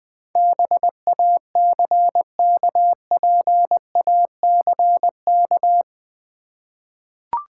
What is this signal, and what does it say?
Morse code, heard as a keyed tone, about 20 wpm, 700 Hz. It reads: BACKPACK
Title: Easy Words with Hints - 20wpm